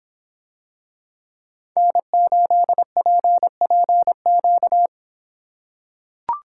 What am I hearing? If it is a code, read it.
Morse code, heard as a keyed tone, about 24 wpm, 700 Hz. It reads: N8PPQ